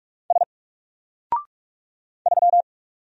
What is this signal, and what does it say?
Morse code, heard as a keyed tone, about 44 wpm, 700 Hz. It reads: S 3
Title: Single Character (Letter:Number:Punctuation:Prosign) Alt 45wpm